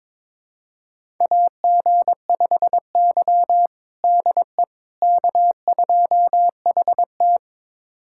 Morse code, heard as a keyed tone, about 22 wpm, 700 Hz. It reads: AG5Y DE K2HT
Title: USA Callsign DE Callsign - 4 Characters - 22wpm